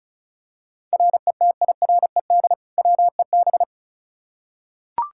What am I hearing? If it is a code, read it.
Morse code, heard as a keyed tone, about 36 wpm, 700 Hz. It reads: RETIRED WEB
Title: Practice QSO - Exchange 3 - More Info - 35wpm